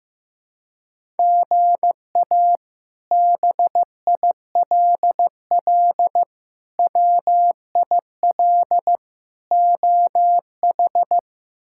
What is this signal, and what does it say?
Morse code, heard as a keyed tone, about 15 wpm, 700 Hz. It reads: GA BILL WIL OH
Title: K1USN SST QSO Practice - 15wpm